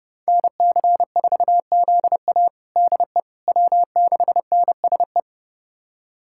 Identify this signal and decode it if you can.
Morse code, heard as a keyed tone, about 30 wpm, 700 Hz. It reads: NC4ZA DE W6NSE
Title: USA Callsign DE Callsign - 5 Characters - 30wpm